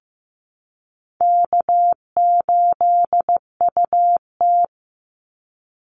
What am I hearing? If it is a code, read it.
Morse code, heard as a keyed tone, about 15 wpm, 700 Hz. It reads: K8UT
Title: Easy Call Signs Repeated 2x - 15wpm